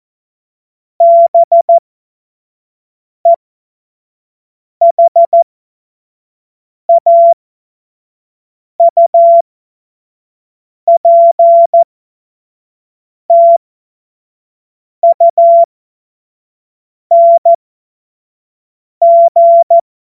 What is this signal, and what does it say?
Morse code, heard as a keyed tone, about 14 wpm, spaced as for 5 wpm, 680 Hz. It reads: BEHAUPTUNG
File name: Morsekurs / Witze / Witz 05. April 2022 25 F70